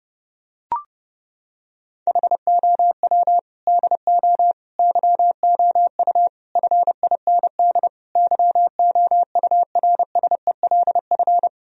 Morse code, heard as a keyed tone, about 30 wpm, 700 Hz. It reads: HOW DO YOU FIND YOURSELF
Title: Sentences from Top 1000 Words - Review Part 4 - 30wpm